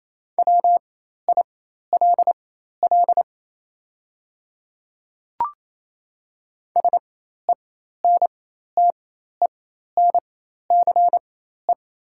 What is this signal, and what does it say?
Morse code, heard as a keyed tone, about 28 wpm, spaced as for 13 wpm, 700 Hz. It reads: WILL SENTENCE
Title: 4x Character Spacing - Top 300 Words - 28wpm